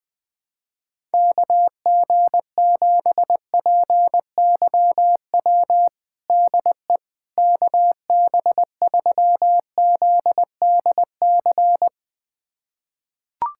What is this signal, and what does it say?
Morse code, heard as a keyed tone, about 20 wpm, 700 Hz. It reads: KG7PYW DE KB3ZDC
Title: USA Callsign DE Callsign - 6 Characters - 20wpm